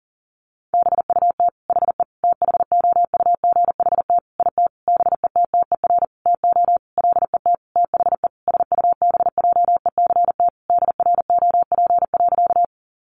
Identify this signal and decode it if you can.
Morse code, heard as a keyed tone, about 40 wpm, 700 Hz. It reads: BUT HE THOUGHT IT BETTER TO LET THE SUBJECT DROP.
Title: Sentences from Top 1400 Words 40wpm